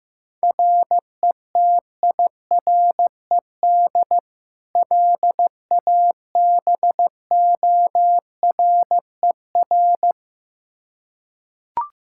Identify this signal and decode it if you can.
Morse code, heard as a keyed tone, about 16 wpm, 700 Hz. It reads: RETIRED LABORER
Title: Practice QSO - Exchange 3 - More Info - 15wpm